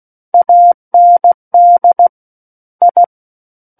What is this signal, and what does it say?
Morse code, heard as a keyed tone, about 16 wpm, 700 Hz. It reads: AND I
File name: Book One / 07-Chapter 7 - 16 WPM